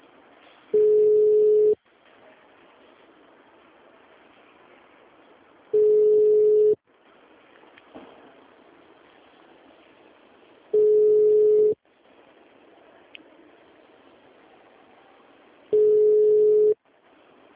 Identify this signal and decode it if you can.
Ringback tone (Europe, 425 Hz).